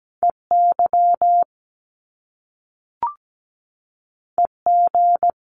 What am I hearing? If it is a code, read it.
Morse code, heard as a keyed tone, about 17 wpm, 700 Hz. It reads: EY EG
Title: Top Two Letter Combinations in Words 17wpm